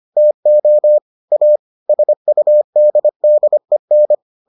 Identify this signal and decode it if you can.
Morse code, heard as a keyed tone, about 25 wpm, 600 Hz. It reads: TO A SUDDEN